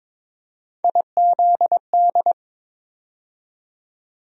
Morse code, heard as a keyed tone, about 22 wpm, 700 Hz. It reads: IZD
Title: USA Call Sign Postfixes - 3 Letters 22wpm